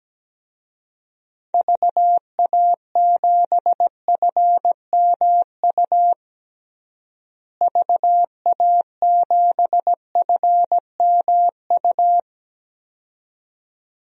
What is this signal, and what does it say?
Morse code, heard as a keyed tone, about 17 wpm, 700 Hz. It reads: VA7FMU VA7FMU
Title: DXCC-100 Call Signs - 6 Characters - Repeated 2x - 17wpm